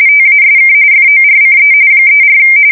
RTTY, about 45.45 baud. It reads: TASK FORCE 34
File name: fsk170 / USS-New-Jersey